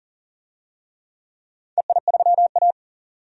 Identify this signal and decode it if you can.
Morse code, heard as a keyed tone, about 40 wpm, 700 Hz. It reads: EI3A
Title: DXCC-100 Call Signs - 4 Characters - 40wpm